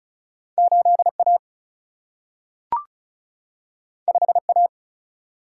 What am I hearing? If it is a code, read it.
Morse code, heard as a keyed tone, about 36 wpm, 700 Hz. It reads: 8A 5A
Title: Single-Letter Single-Number 35wpm